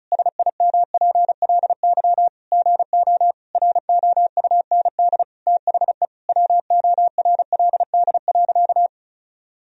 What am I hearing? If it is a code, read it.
Morse code, heard as a keyed tone, about 36 wpm, 700 Hz. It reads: SIMPLY GO ROUND THE WORLD.
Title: Sentences from Top 1000 Words - Review Part 4 - 35wpm